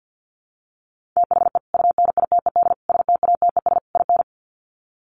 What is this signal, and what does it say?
Morse code, heard as a keyed tone, about 50 wpm, 700 Hz. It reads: THE UNITED STATES IN
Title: Sets of 4 Words 50wpm